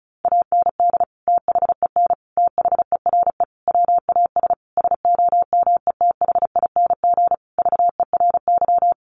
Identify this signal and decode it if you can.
Morse code, heard as a keyed tone, about 34 wpm, 700 Hz. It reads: AND THEN THERE WAS SOMETHING VERY
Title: Sentences from Top 1400 Words 35wpm